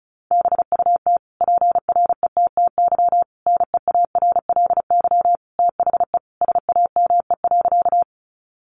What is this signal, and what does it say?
Morse code, heard as a keyed tone, about 34 wpm, 700 Hz. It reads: BUT PRETTY NEARLY THE SAME.
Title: Sentences from Top 1300 Words 35wpm